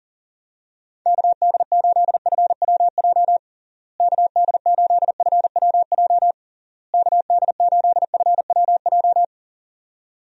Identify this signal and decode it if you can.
Morse code, heard as a keyed tone, about 40 wpm, 700 Hz. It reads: KD8FWJ KD8FWJ KD8FWJ
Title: USA Call Signs - 6 Characters - Repeated 3x - 40wpm